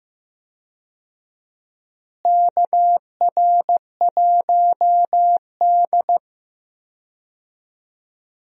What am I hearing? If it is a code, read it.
Morse code, heard as a keyed tone, about 15 wpm, 700 Hz. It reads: KR1D
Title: USA Call Signs - 4 Characters - Repeated 3x - 15wpm